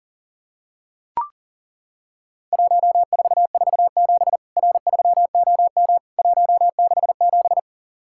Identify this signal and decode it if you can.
Morse code, heard as a keyed tone, about 40 wpm, 700 Hz. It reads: 1447 R3OM 167
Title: Worked All Europe DX Contest - QTC Exchange - 40wpm